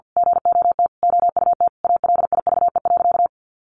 Morse code, heard as a keyed tone, about 50 wpm, 700 Hz. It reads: GOT OUT ALIVE.
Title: A Sound Of Thunder 50wpm